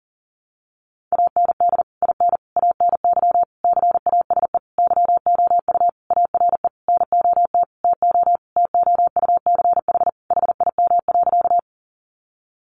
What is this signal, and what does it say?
Morse code, heard as a keyed tone, about 40 wpm, 700 Hz. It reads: AND IN ANY CASE YOU ARE NOT TO TOUCH HIM.
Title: Sentences from Top 1200 Words 40wpm